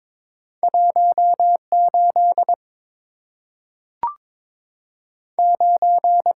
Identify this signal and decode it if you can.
Morse code, heard as a keyed tone, about 22 wpm, 690 Hz. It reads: 18 9
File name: practice / 22wpm / Two Letter-Number Combinations - 22wpm